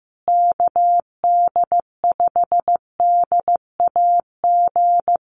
Morse code, heard as a keyed tone, about 15 wpm, 700 Hz. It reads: KD5DAG
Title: USA Call Signs Repeated 2x 15wpm